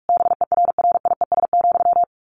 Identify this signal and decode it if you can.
Morse code, heard as a keyed tone, about 44 wpm, 700 Hz. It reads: BERRIES,
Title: A Sound Of Thunder 45wpm